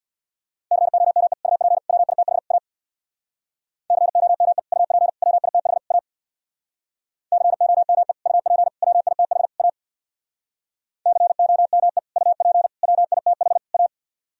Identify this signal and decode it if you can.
Morse code, heard as a keyed tone, about 64 wpm, 700 Hz. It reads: COME UP WITH A COME UP WITH A COME UP WITH A COME UP WITH A